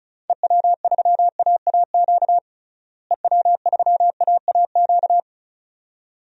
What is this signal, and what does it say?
Morse code, heard as a keyed tone, about 34 wpm, 700 Hz. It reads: EW3AAQ EW3AAQ
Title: DXCC-100 Call Signs - 6 Characters - Repeated 2x - 35wpm